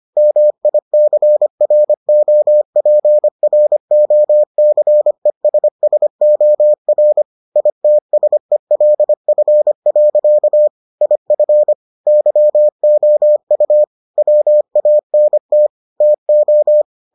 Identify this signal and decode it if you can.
Morse code, heard as a keyed tone, about 25 wpm, 600 Hz. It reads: MICROPROCESSOR ITSELF. IF YOU WANT TO